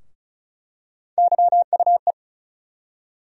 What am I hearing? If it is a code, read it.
Morse code, heard as a keyed tone, about 35 wpm, 700 Hz. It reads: YUE
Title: USA Call Sign Postfixes - 3 Letters 35wpm